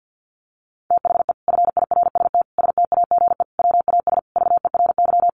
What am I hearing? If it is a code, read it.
Morse code, heard as a keyed tone, about 50 wpm, 700 Hz. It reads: THE FIRST STAGE WAS VERY
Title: Sentences from Top 1200 Words 50wpm